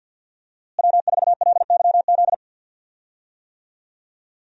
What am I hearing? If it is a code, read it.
Morse code, heard as a keyed tone, about 51 wpm, 700 Hz. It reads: W4LYZ
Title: USA Call Signs - 5 Characters - Repeated 3x - 50wpm